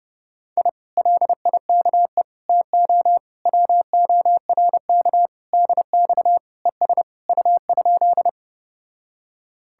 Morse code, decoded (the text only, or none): I LIKE TO WORK DX ES U?